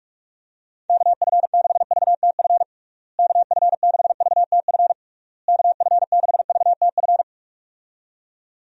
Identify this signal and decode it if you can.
Morse code, heard as a keyed tone, about 46 wpm, 700 Hz. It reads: KF6VTF KF6VTF KF6VTF
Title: USA Call Signs - 6 Characters - Repeated 3x - 45wpm